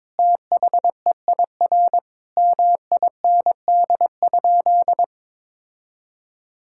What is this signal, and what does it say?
Morse code, heard as a keyed tone, about 22 wpm, 700 Hz. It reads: THEIR MIND?
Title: Sentences from Top 1000 Words - Review Part 5 - 22wpm